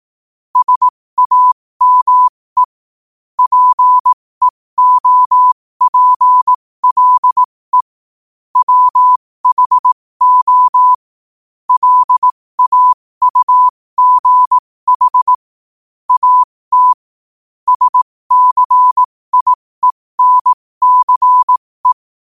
Morse code, heard as a keyed tone, about 18 wpm, spaced as for 15 wpm, 1000 Hz. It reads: SAME PEOPLE WHO LAUGH AT SCIENCE
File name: qotdfiles / 15wpm